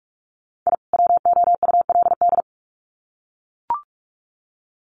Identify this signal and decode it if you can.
Morse code, heard as a keyed tone, about 44 wpm, 700 Hz. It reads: I WOULD
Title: Sets of 2 Words 45wpm